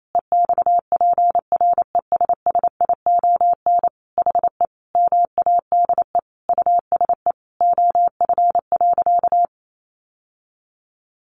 Morse code, heard as a keyed tone, about 28 wpm, 700 Hz. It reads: EXPRESSION HE MADE USE OF.
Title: Sentences from Top 1200 Words 28wpm